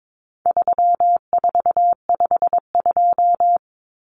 Morse code, heard as a keyed tone, about 22 wpm, 700 Hz. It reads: 3452
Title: Numbers - 4-Digit 22wpm